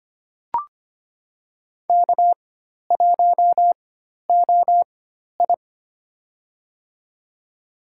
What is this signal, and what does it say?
Morse code, heard as a keyed tone, about 25 wpm, spaced as for 12 wpm, 700 Hz. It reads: K1OI
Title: USA Call Signs - 4 Characters - 4x Character Spacing - 25wpm